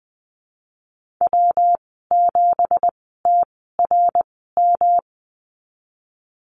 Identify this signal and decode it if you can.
Morse code, heard as a keyed tone, about 20 wpm, spaced as for 14 wpm, 700 Hz. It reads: W7TRM
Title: 173 Lesson 28g - Number 9 - Call Signs 20wpm